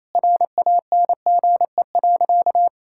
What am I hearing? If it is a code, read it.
Morse code, heard as a keyed tone, about 28 wpm, 700 Hz. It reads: RANGE.
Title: Sentences from Top 1000 Words - Review Part 6 - 28wpm